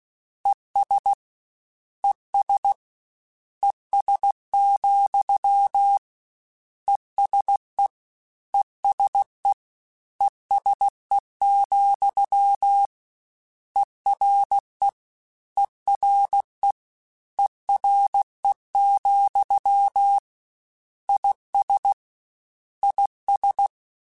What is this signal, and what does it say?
Morse code, heard as a keyed tone, about 16 wpm, 800 Hz. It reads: ES ES ES, ESE ESE ESE, ERE ERE ERE, IS IS
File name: General / General-mp3s / General @15